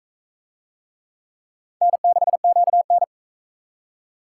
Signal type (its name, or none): Morse code